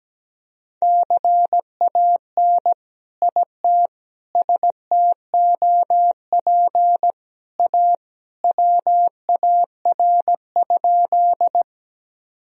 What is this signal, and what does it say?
Morse code, heard as a keyed tone, about 17 wpm, 700 Hz. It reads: CAN IT STOP A WAR?
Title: Sentences from Top 1000 Words - Review Part 2 - 17wpm